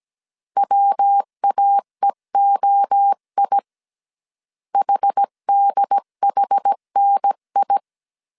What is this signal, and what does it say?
Morse code, heard as a keyed tone, about 17 wpm, 800 Hz. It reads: WAEOI HDHNI
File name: CW / DET / 1 3 1500C S A 16wpm 2141